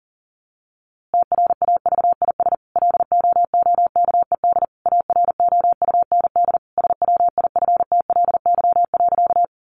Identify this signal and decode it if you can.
Morse code, heard as a keyed tone, about 37 wpm, 700 Hz. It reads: TRAVIS LOOKED AROUND SWIFTLY.
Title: A Sound Of Thunder 40wpm